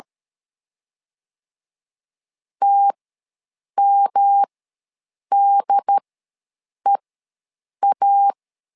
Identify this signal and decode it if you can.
Morse code, heard as a keyed tone, about 13 wpm, spaced as for 7 wpm, 800 Hz. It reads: TMDEA
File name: LEZIONI / 3 MWRO / 3 M ws 12wpm 1745